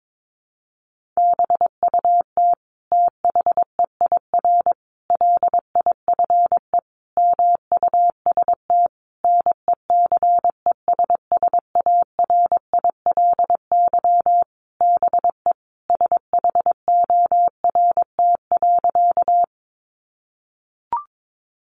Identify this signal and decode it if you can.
Morse code, heard as a keyed tone, about 22 wpm, 700 Hz. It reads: BUT THEIR LIFE MUST NECESSARILY BE SHORT.
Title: Sentences from Top 1400 Words 22wpm